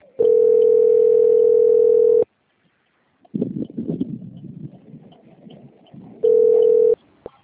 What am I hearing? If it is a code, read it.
Ringback tone (North America), 440 + 480 Hz.
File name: out- - - - - .32023